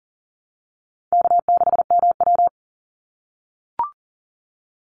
Morse code, heard as a keyed tone, about 40 wpm, 700 Hz. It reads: K6MW
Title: NA SOTA Chaser Callsigns 40wpm